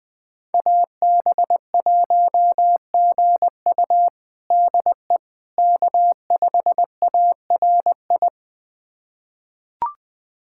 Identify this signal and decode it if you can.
Morse code, heard as a keyed tone, about 20 wpm, 700 Hz. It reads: AB1GU DE K5ARI
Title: USA Callsign DE Callsign - 5 Characters - 20wpm